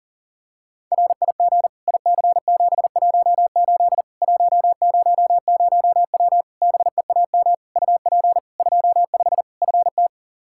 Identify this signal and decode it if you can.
Morse code, heard as a keyed tone, about 40 wpm, 700 Hz. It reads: RIG IC718 100W BEAM UP 25 FT
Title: Practice QSO - Exchange 2 - Info - 40wpm